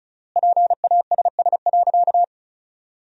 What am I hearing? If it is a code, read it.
Morse code, heard as a keyed tone, about 36 wpm, 700 Hz. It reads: PASS.
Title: 3x Word-Spacing - Sentences from Top 400 Words - 35wpm